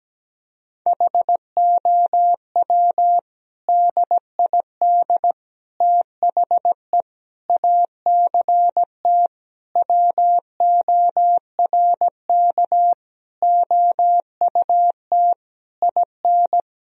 Morse code, heard as a keyed tone, about 17 wpm, 700 Hz. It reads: HOW DID THE ACT WORK OUT IN
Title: Sentences from Top 1000 Words - Review Part 4 - 17wpm